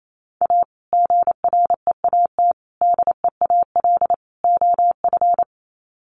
Morse code, heard as a keyed tone, about 28 wpm, 700 Hz. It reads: A GREAT DEAL OF
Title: Sets of 4 Words 28wpm